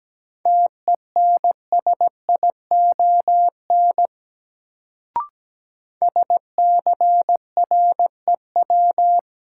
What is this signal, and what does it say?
Morse code, heard as a keyed tone, about 17 wpm, 700 Hz. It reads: TENSION SCREW
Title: Sentences from Top 1600 Words - New Word Drill - 17wpm